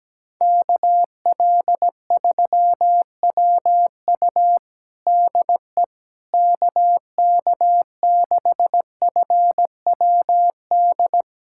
Morse code, heard as a keyed tone, about 17 wpm, 700 Hz. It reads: KL3WU DE KK6FWD
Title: USA Callsign DE Callsign 17wpm